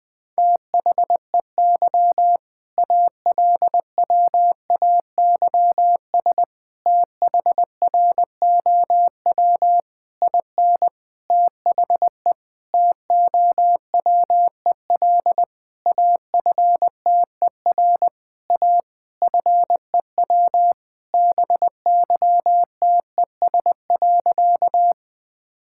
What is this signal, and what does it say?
Morse code, heard as a keyed tone, about 20 wpm, 700 Hz. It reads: THEY ALWAYS THROW IN THE TOWEL AFTER A FEW BYTES.
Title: All-Age Appropriate Jokes - 20wpm